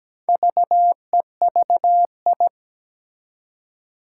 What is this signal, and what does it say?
Morse code, heard as a keyed tone, about 17 wpm, 700 Hz. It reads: VEVI
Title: Four Challenging Letter-Number Combinations 17wpm